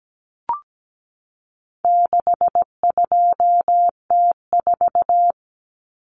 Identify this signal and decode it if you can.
Morse code, heard as a keyed tone, about 17 wpm, 700 Hz. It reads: 62T4
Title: Common Cut Numbers - 4-Digit 17wpm